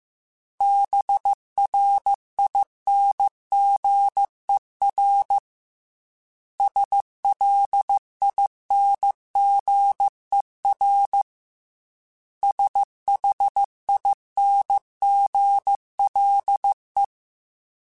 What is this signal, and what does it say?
Morse code, heard as a keyed tone, about 15 wpm, 800 Hz. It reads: BRINGER SLINGER SHINGLE